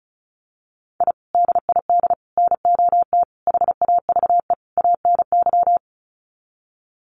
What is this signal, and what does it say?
Morse code, heard as a keyed tone, about 34 wpm, 700 Hz. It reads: I DID NOT HAVE ANY
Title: Sets of 5 Words Encore 35wpm